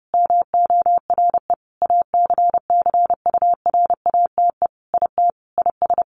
Morse code, heard as a keyed tone, about 30 wpm, 700 Hz. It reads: MORE ACCURATE IT IS
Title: Sentences from Top 1200 Words 30wpm